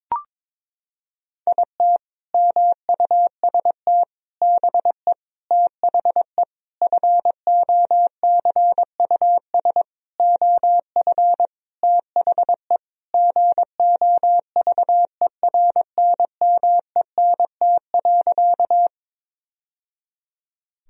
Morse code, heard as a keyed tone, about 22 wpm, 700 Hz. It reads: IT MUST BE THE FOCUS OF THE GOVERNMENT.
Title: Sentences from Top 1000 Words - Review Part 4 - 22wpm